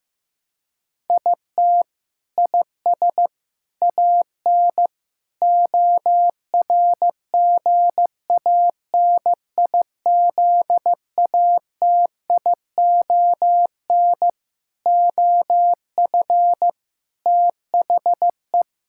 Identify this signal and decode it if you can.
Morse code, heard as a keyed tone, about 15 wpm, 700 Hz. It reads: IT IS AN ORGANIZATION OF THE
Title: Sentences from Top 1000 Words - Review Part 6 - 15wpm